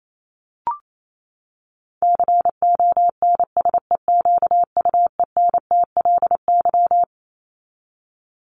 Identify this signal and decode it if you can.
Morse code, heard as a keyed tone, about 28 wpm, 700 Hz. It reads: CONSEQUENTLY
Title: LY Suffix - Top 100 Words with Base Words 28wpm